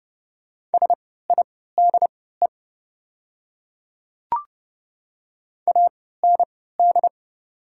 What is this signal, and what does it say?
Morse code, heard as a keyed tone, about 30 wpm, spaced as for 17 wpm, 700 Hz. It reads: SIDE AND
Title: 3x Character Spacing - Top 100 Words - 30wpm